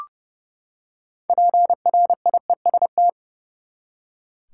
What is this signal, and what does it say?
Morse code, heard as a keyed tone, about 30 wpm, 700 Hz. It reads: PRIEST
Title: Six Letter Words - 30wpm